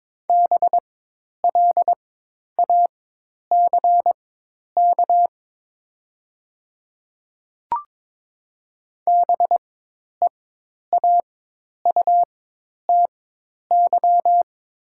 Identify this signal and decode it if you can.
Morse code, heard as a keyed tone, about 22 wpm, spaced as for 10 wpm, 700 Hz. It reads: BLACK BEAUTY
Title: 4x Character Spacing - Top 500 Words - 22wpm